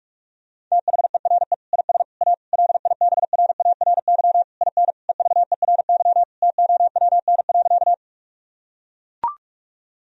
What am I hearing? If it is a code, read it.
Morse code, heard as a keyed tone, about 44 wpm, 700 Hz. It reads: THERE IS A LIBRARY IN EVERY TOWN.
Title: Sentences from Top 1000 Words - Review Part 4 - 45wpm